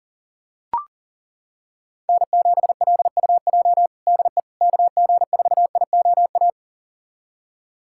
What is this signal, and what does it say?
Morse code, heard as a keyed tone, about 40 wpm, 700 Hz. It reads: N7LUJ DE KG4IOA
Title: USA Callsign DE Callsign 40wpm